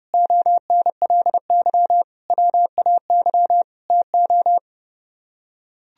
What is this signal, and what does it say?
Morse code, heard as a keyed tone, about 29 wpm, 700 Hz. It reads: ONLY WAY TO